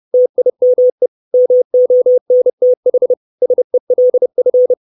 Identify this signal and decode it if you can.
Morse code, heard as a keyed tone, about 30 wpm, 500 Hz. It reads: TIME MONTH SELF
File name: 632 Words 30WPM